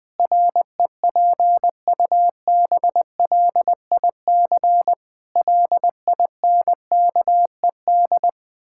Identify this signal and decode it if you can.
Morse code, heard as a keyed tone, about 20 wpm, 700 Hz. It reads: REPUBLIC LINKED
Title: Pangram Challenge v2 - 20wpm